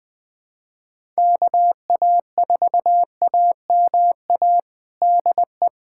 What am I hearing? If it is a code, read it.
Morse code, heard as a keyed tone, about 21 wpm, 700 Hz. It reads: KA4AMA DE
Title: USA Callsign DE Callsign 20wpm